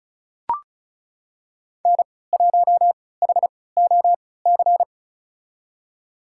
Morse code, heard as a keyed tone, about 36 wpm, spaced as for 20 wpm, 700 Hz. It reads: N1HOC
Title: USA Call Signs - 5 Characters - 3x Character Spacing - 35wpm